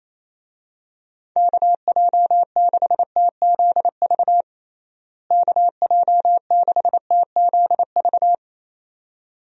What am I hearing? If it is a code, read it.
Morse code, heard as a keyed tone, about 28 wpm, 700 Hz. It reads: KJ6TZV KJ6TZV
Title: USA Call Signs - 6 Characters - Repeated 2x - 28wpm